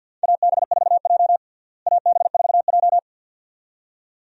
Morse code, heard as a keyed tone, about 52 wpm, 700 Hz. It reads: AB4J AB4J
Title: USA Call Signs - 4 Characters - Repeated 2x - 50wpm